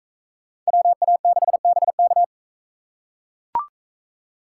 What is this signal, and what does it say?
Morse code, heard as a keyed tone, about 39 wpm, 700 Hz. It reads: WA6BK